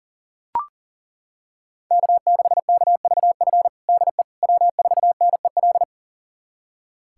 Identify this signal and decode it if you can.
Morse code, heard as a keyed tone, about 40 wpm, 700 Hz. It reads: K6KVF DE W4NEL
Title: USA Callsign DE Callsign - 5 Characters - 40wpm